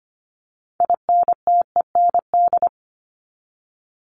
Morse code, heard as a keyed tone, about 25 wpm, 700 Hz. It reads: INTEND
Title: Top 1501-1750 Words 25wpm